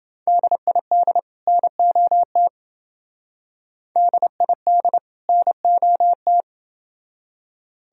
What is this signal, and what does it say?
Morse code, heard as a keyed tone, about 28 wpm, 700 Hz. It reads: DID NOT DID NOT